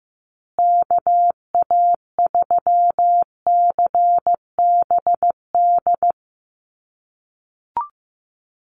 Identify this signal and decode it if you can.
Morse code, heard as a keyed tone, about 15 wpm, 700 Hz. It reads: KA3CBD
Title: USA Call Signs Repeated 3x 15wpm